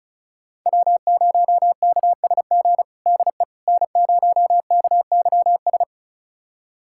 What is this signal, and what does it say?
Morse code, heard as a keyed tone, about 36 wpm, 700 Hz. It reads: W0KSG DE N0KYS
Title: USA Callsign DE Callsign - 5 Characters - 35wpm